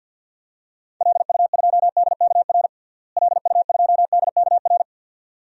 Morse code, heard as a keyed tone, about 46 wpm, 700 Hz. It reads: LU2DKR LU2DKR
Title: DXCC-100 Call Signs - 6 Characters - Repeated 2x - 50wpm